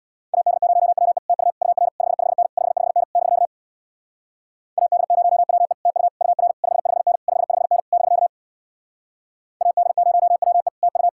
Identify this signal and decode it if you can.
Morse code, heard as a keyed tone, about 66 wpm, 700 Hz. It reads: AD0WE TU UR 55N 55N <BK> AD0WE TU UR 55N 55N <BK> AD0WE TU